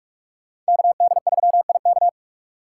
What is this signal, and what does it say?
Morse code, heard as a keyed tone, about 57 wpm, 700 Hz. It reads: KD3IK